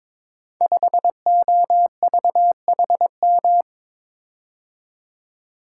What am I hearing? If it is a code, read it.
Morse code, heard as a keyed tone, about 22 wpm, 690 Hz. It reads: 5OVHM
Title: Code Groups - 5 Letter:Numbers - 22wpm